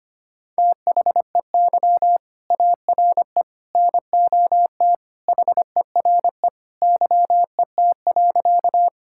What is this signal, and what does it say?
Morse code, heard as a keyed tone, about 25 wpm, 700 Hz. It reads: THEY ARE NOT HERE YET.
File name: practice / 25wpm / Sentences with 2-Word Phrases - 25wpm